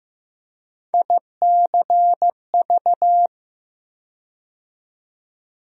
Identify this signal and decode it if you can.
Morse code, heard as a keyed tone, about 15 wpm, 700 Hz. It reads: ICV
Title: USA Call Sign Postfixes - 3 Letters 15wpm